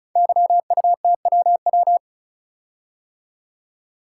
Morse code, heard as a keyed tone, about 34 wpm, 700 Hz. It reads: YUTWW
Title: Code Groups - 5 Letters - 35wpm